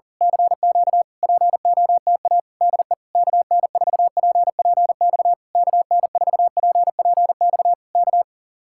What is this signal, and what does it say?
Morse code, heard as a keyed tone, about 35 wpm, 700 Hz. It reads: CQ POTA DE KN4PPX KN4PPX K
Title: Speed-Racing Common QSO Phrases Encore - 40wpm